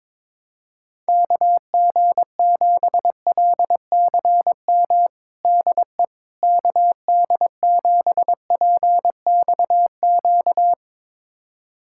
Morse code, heard as a keyed tone, about 22 wpm, 700 Hz. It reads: KG7LCM DE KD7PXQ
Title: USA Callsign DE Callsign - 6 Characters - 22wpm